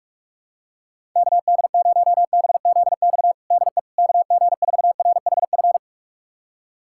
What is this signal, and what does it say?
Morse code, heard as a keyed tone, about 43 wpm, 700 Hz. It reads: KD0BZX DE KG4RHF
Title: USA Callsign DE Callsign - 6 Characters - 45wpm